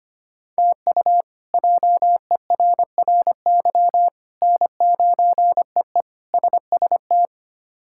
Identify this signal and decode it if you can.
Morse code, heard as a keyed tone, about 25 wpm, 700 Hz. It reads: TU JERRY N9EE SST
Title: K1USN SST QSO Practice - 25wpm